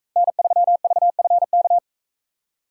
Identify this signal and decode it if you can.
Morse code, heard as a keyed tone, about 41 wpm, 700 Hz. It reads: N3VFK